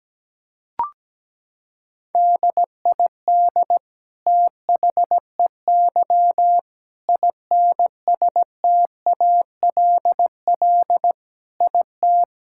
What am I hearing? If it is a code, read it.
Morse code, heard as a keyed tone, about 17 wpm, 700 Hz. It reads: DID THEY INSTALL IT
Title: Sentences from Top 1600 Words 17wpm